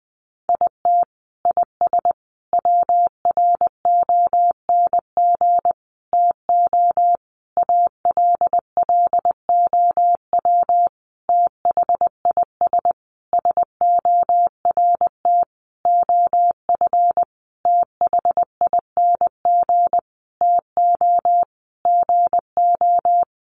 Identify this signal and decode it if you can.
Morse code, heard as a keyed tone, about 20 wpm, 700 Hz. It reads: IT IS WRONG TO ALLOW THIS SORT OF THING TO GO
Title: Sentences from Top 1300 Words 20wpm